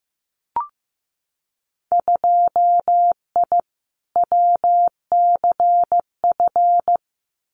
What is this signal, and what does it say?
Morse code, heard as a keyed tone, about 15 wpm, 700 Hz. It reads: 2I WCF
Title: Winter Field Day Exchanges 15wpm